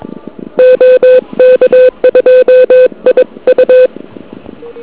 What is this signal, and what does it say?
Morse code, heard as a keyed tone, about 22 wpm, 1500 Hz. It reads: OK2IU
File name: ARCHIV / KX3 / KV PA